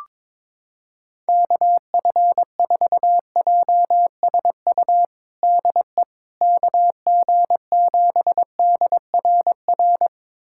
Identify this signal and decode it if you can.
Morse code, heard as a keyed tone, about 22 wpm, 700 Hz. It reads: KF4JSU DE KG7DRR
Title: USA Callsign DE Callsign 22wpm